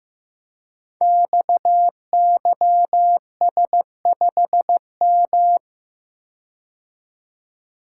Morse code, heard as a keyed tone, about 15 wpm, 700 Hz. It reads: XYS5M
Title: Code Groups - 5 Letter:Numbers - 15wpm